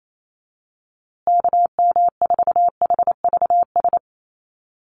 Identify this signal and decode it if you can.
Morse code, heard as a keyed tone, about 28 wpm, 700 Hz. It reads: KM4HVS
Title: USA Call Signs Repeated 2x 28wpm